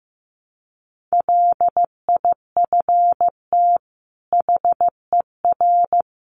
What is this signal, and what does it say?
Morse code, heard as a keyed tone, about 15 wpm, 700 Hz. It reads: LIFT HER
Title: Sentences from Top 1300 Words 15wpm